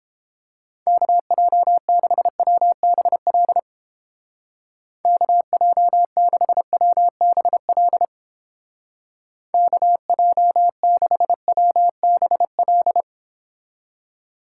Morse code, heard as a keyed tone, about 30 wpm, 700 Hz. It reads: KJ6WBL KJ6WBL KJ6WBL